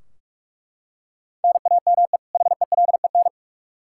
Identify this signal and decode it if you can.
Morse code, heard as a keyed tone, about 42 wpm, 700 Hz. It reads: NAME HELEN
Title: Practice QSO - Exchange 1 - Beginning - 45wpm